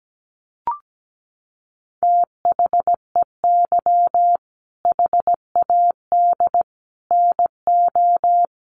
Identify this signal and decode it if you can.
Morse code, heard as a keyed tone, about 17 wpm, 700 Hz. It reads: THEY HAD NO
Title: Sentences from Top 1500 Words 17wpm